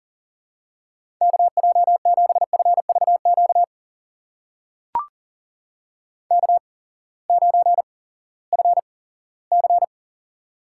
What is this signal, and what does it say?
Morse code, heard as a keyed tone, about 40 wpm, spaced as for 20 wpm, 700 Hz. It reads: KJ7FVQ K9FC
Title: USA Call Signs - 8x Character Spacing - 40wpm